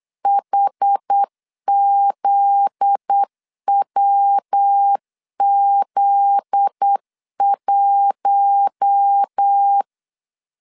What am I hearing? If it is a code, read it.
Morse code, heard as a keyed tone, about 8 wpm, 800 Hz. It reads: HZWZ1
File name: LEZIONI / 7 GF9Z / 7 GF9Z 8wpm 3213